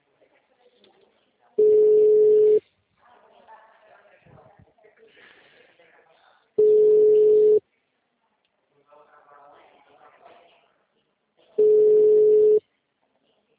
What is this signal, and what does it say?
Ringback tone (Europe, 425 Hz), with speech.